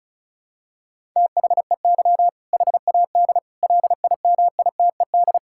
Morse code, heard as a keyed tone, about 36 wpm, 700 Hz. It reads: THEY HAD LIMITED
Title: Sentences from Top 1000 Words - Review Part 6 - 35wpm